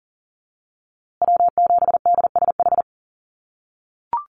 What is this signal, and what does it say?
Morse code, heard as a keyed tone, about 40 wpm, 700 Hz. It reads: W7DSH
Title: USA Call Signs Repeated 3x 40wpm